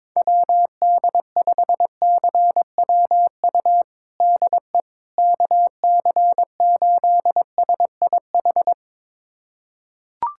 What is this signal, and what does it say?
Morse code, heard as a keyed tone, about 22 wpm, 700 Hz. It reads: WD5CWU DE KC8SIH
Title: USA Callsign DE Callsign 22wpm